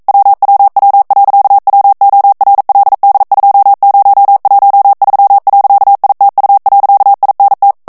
Morse code, heard as a keyed tone, about 42 wpm, 800 Hz. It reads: WWW.WORLD2013.ITU.INT
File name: morse mp3 / ESPAGNOL TEXTE EN CLAIR 1 VITESSE 40 MOTS MINUTES